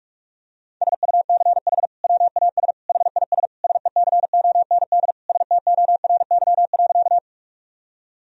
Morse code, heard as a keyed tone, about 46 wpm, 700 Hz. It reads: SUCH WAS HIS SECOND STORY.
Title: Sentences from Top 1000 Words - Review Part 1 - 45wpm